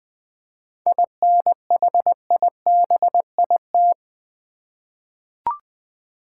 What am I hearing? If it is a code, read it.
Morse code, heard as a keyed tone, about 20 wpm, 700 Hz. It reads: INHIBIT
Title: 7 Letter Words - First 3 Letters Given - 20wpm